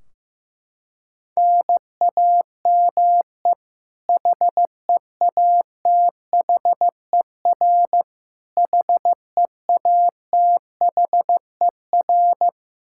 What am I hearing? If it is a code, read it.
Morse code, heard as a keyed tone, about 15 wpm, 700 Hz. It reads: NAME HEATHER HEATHER
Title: K1USN SST QSO Practice 15wpm